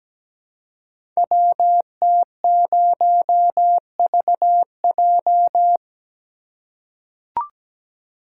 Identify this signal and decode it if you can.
Morse code, heard as a keyed tone, about 17 wpm, 700 Hz. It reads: WT0VJ
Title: Code Groups - 5 Letter:Numbers - 17wpm